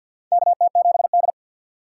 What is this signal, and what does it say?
Morse code, heard as a keyed tone, about 52 wpm, 700 Hz. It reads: KT7D
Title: USA Call Signs - 4x Character Spacing - 50wpm